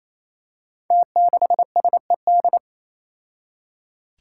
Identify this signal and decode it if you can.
Morse code, heard as a keyed tone, about 27 wpm, 700 Hz. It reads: T6SED
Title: Five Challenging Letter-Number Combinations 28wpm